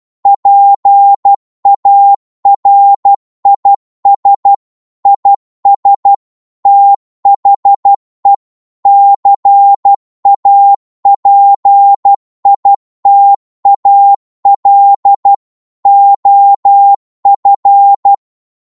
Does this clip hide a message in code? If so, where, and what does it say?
Yes: Morse code from 0.3 to 18.1 s, heard as a keyed tone, about 12 wpm, 800 Hz. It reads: PARIS IS THE CAPITAL OF